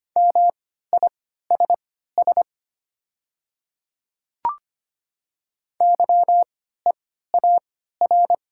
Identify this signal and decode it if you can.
Morse code, heard as a keyed tone, about 25 wpm, spaced as for 14 wpm, 690 Hz. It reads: MISS YEAR
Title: 3x Character Spacing - Top 500 Words - 25wpm